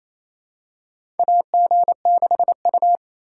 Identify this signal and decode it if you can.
Morse code, heard as a keyed tone, about 28 wpm, 700 Hz. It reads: AG6U
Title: USA Call Signs - 2x Character Spacing - 28wpm